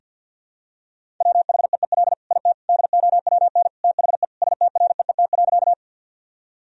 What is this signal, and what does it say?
Morse code, heard as a keyed tone, about 50 wpm, 700 Hz. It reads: WHEEL IT DOWN THE STREET.
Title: Sentences from Top 1000 Words - Review Part 3 - 50wpm